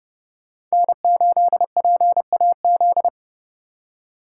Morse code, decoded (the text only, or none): N8PAZ